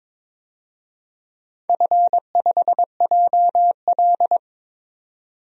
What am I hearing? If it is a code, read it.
Morse code, heard as a keyed tone, about 23 wpm, 700 Hz. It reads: F5JL
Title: DXCC-100 Call Signs - 4 Characters - 22wpm